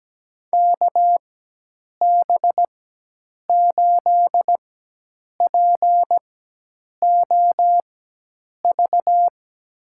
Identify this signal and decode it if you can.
Morse code, heard as a keyed tone, about 17 wpm, spaced as for 8 wpm, 700 Hz. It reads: KB8POV
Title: USA Call Signs - 6 Characters - 4x Character Spacing - 17wpm